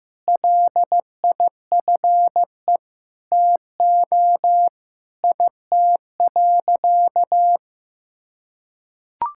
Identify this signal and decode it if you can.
Morse code, heard as a keyed tone, about 15 wpm, 700 Hz. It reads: LIFE TO IT.
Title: Sentences from Top 1000 Words - Review Part 5 - 15wpm